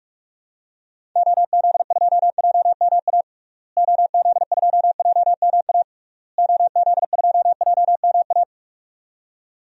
Morse code, heard as a keyed tone, about 42 wpm, 700 Hz. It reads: OZ2JMA OZ2JMA OZ2JMA
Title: DXCC-100 Call Signs - 6 Characters - Repeated 3x - 45wpm